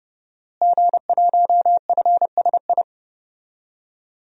Morse code, heard as a keyed tone, about 31 wpm, 700 Hz. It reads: G1FSI
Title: DXCC-100 Call Signs - 5 Characters - Repeated 2x - 30wpm